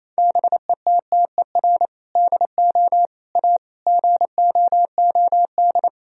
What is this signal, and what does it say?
Morse code, heard as a keyed tone, about 28 wpm, 700 Hz. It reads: BETTER DO A GOOD
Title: Sentences from Top 1000 Words - Review Part 3 - 28wpm